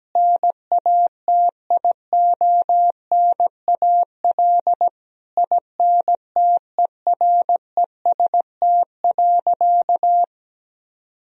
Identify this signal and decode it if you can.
Morse code, heard as a keyed tone, about 17 wpm, 700 Hz. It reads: NATIONAL INTEREST.
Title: Sentences from Top 1000 Words - Review Part 4 - 17wpm